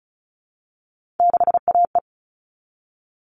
Morse code, heard as a keyed tone, about 34 wpm, 700 Hz. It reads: 6AE
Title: All Cut Numbers - 3-Digit 35wpm